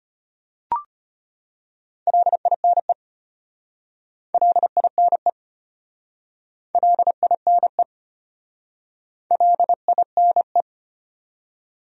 Morse code, heard as a keyed tone, about 34 wpm, 700 Hz. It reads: LINE LINE LINE LINE